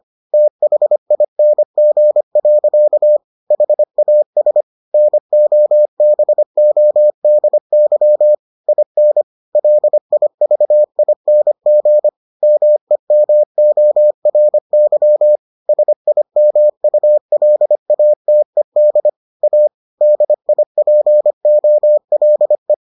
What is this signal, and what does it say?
Morse code, heard as a keyed tone, about 25 wpm, 600 Hz. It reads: THING. HAS NOBODY IN LIVING MEMORY SIMULATED A DIPOLE